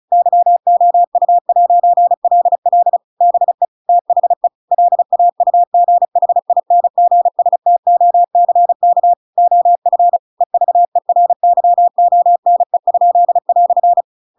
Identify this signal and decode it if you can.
Morse code, heard as a keyed tone, about 34 wpm, 700 Hz. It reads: YOU'LL BE THE LAUGHINGSTOCK OF EVERYONE?"